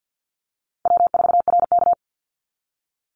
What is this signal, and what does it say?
Morse code, heard as a keyed tone, about 50 wpm, 700 Hz. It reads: W4RK
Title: NA SOTA Chaser Callsigns 50wpm